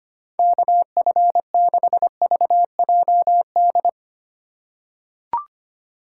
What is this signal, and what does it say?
Morse code, heard as a keyed tone, about 25 wpm, 700 Hz. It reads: KF6VJD
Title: USA Call Signs - 6 Characters - 3x Character Spacing - 25wpm